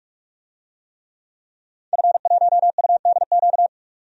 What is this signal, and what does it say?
Morse code, heard as a keyed tone, about 46 wpm, 700 Hz. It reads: F1UDQ
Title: DXCC-100 Call Signs - 5 Characters - 45wpm